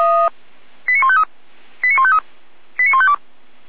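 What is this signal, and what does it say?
Five-tone selective call (ZVEI), heard as quick runs of five tones, tones of about 70 ms. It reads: 89153 89153 89152
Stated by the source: ZVEI five-tone call received on an SDR, from the Feuerwehr Orlamünde fire brigade.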